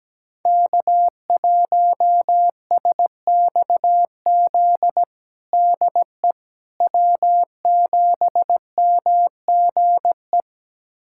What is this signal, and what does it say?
Morse code, heard as a keyed tone, about 17 wpm, 700 Hz. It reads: K1SXZ DE W7MGE
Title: USA Callsign DE Callsign - 5 Characters - 17wpm